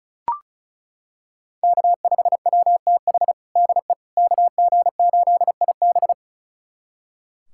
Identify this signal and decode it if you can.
Morse code, heard as a keyed tone, about 36 wpm, 700 Hz. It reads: K5WTH DE KG8IB
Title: USA Callsign DE Callsign - 5 Characters - 35wpm